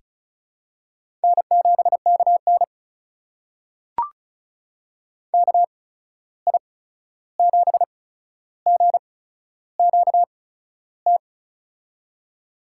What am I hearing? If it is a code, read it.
Morse code, heard as a keyed tone, about 36 wpm, spaced as for 13 wpm, 700 Hz. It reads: N7KN KI7GQT
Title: USA Call Signs - 8x Character Spacing - 35wpm